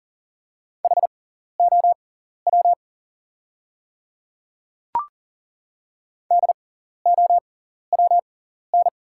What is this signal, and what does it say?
Morse code, heard as a keyed tone, about 37 wpm, spaced as for 14 wpm, 690 Hz. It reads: HOW DOWN
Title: 6x Character Spacing - Top 200 Words - 40wpm